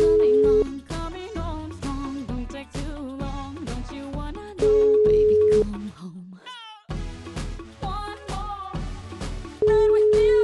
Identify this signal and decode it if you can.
Ringback tone (Europe, 425 Hz), with speech.